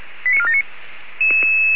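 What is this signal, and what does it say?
Five-tone selective call (ZVEI), heard as quick runs of five tones, tones of about 70 ms. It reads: 89479
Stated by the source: ZVEI five-tone call 85997 received on an SDR.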